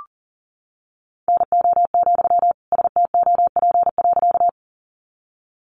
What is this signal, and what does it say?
Morse code, heard as a keyed tone, about 40 wpm, 700 Hz. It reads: NO, STOP.
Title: Sentences from Top 300 Words 40wpm